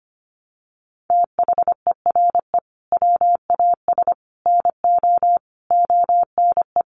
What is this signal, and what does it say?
Morse code, heard as a keyed tone, about 25 wpm, 700 Hz. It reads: THERE WAS NO ONE
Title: Sentences from Top 200 Words 25wpm